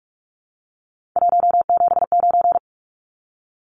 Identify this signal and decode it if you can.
Morse code, heard as a keyed tone, about 44 wpm, 700 Hz. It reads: 179
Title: Numbers - 3-Digit 45wpm